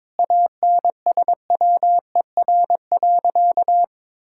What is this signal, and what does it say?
Morse code, heard as a keyed tone, about 22 wpm, 700 Hz. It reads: ANSWER.
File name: practice / 22wpm / Sentences with 4-Word Phrases - 22wpm